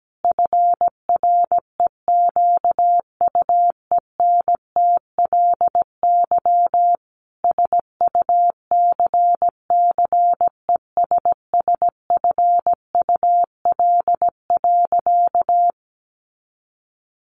Morse code, heard as a keyed tone, about 17 wpm, 700 Hz. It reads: FREQUENTLY SUCCESSFUL.
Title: Sentences from Top 1300 Words 17wpm